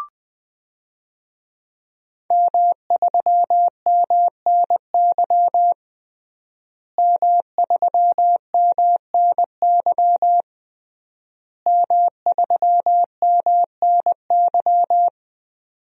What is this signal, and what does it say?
Morse code, heard as a keyed tone, about 21 wpm, 700 Hz. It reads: M3MNY M3MNY M3MNY
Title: DXCC-100 Call Signs - 5 Characters - Repeated 3x - 20wpm